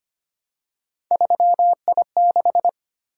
Morse code, heard as a keyed tone, about 25 wpm, 700 Hz. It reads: 3I6
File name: Three Letter-Number Combinations - 25wpm